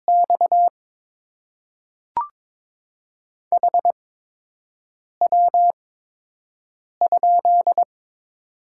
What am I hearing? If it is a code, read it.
Morse code, heard as a keyed tone, about 22 wpm, spaced as for 6 wpm, 700 Hz. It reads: X HW?
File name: Top 20 QSO Elements - 8x Character Spacing - 22wpm